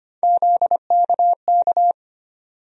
Morse code, heard as a keyed tone, about 25 wpm, 700 Hz. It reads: ZKK
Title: USA Call Sign Postfixes - 3 Letters 25wpm